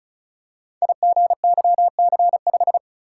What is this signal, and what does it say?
Morse code, heard as a keyed tone, about 36 wpm, 700 Hz. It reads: IGYC5
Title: Code Groups - 5 Letter:Numbers - 35wpm